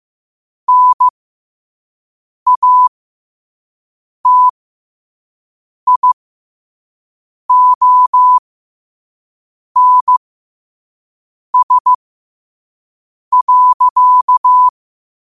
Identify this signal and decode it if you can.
Morse code, heard as a keyed tone, about 15 wpm, spaced as for 5 wpm, 1000 Hz. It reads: NATIONS.
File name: 5wpm